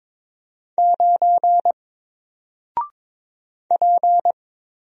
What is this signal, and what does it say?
Morse code, heard as a keyed tone, about 22 wpm, 700 Hz. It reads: 9 P
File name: Single-Letter-Number-22wpm